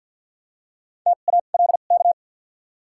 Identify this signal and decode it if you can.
Morse code, heard as a keyed tone, about 48 wpm, spaced as for 35 wpm, 700 Hz. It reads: TALK
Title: 2x Character Spacing - Top 500 Words - 50wpm